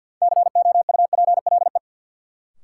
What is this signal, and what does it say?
Morse code, heard as a keyed tone, about 46 wpm, 700 Hz. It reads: COUPLE
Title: Sets of 2 Words - New Word Drill - 50wpm